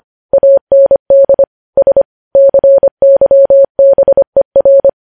Morse code, heard as a keyed tone, about 25 wpm, 550 Hz. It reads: AND S CYBER
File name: podcast / News headlines-1x-25x25